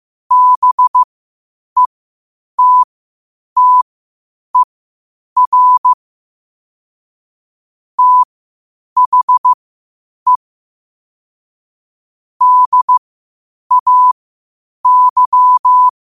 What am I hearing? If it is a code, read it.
Morse code, heard as a keyed tone, about 15 wpm, spaced as for 8 wpm, 1000 Hz. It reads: BETTER THE DAY
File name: qotdfiles / 8wpm